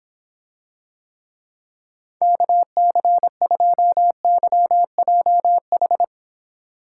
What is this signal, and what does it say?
Morse code, heard as a keyed tone, about 26 wpm, 700 Hz. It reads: KC2YJH